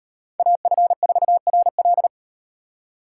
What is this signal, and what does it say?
Morse code, heard as a keyed tone, about 38 wpm, 700 Hz. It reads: AF4RL